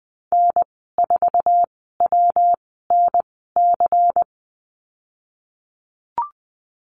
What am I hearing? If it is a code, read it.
Morse code, heard as a keyed tone, about 20 wpm, spaced as for 14 wpm, 700 Hz. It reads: N4WNC
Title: 140 Lesson 23g - Letter f - Call Signs 20wpm